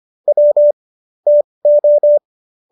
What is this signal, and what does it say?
Morse code, heard as a keyed tone, about 25 wpm, spaced as for 20 wpm, 590 Hz. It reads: W TO